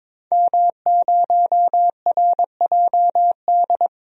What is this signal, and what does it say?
Morse code, heard as a keyed tone, about 22 wpm, 700 Hz. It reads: M0RJD
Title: DXCC-100 Call Signs - 5 Characters - Repeated 3x - 22wpm